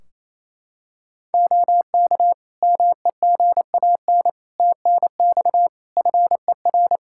Morse code, heard as a keyed tone, about 28 wpm, 700 Hz. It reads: OK MEGAN TNX FER
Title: Practice QSO - Exchange 4 - Ending - 28wpm